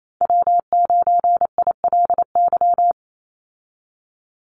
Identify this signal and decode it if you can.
Morse code, heard as a keyed tone, about 28 wpm, 700 Hz. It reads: W9ILY
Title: Call Signs - 28wpm